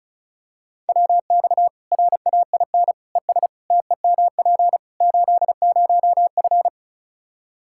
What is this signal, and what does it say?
Morse code, heard as a keyed tone, about 36 wpm, 700 Hz. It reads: WX RAIN ES TEMP 80F
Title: Practice QSO - Exchange 2 - Info - 35wpm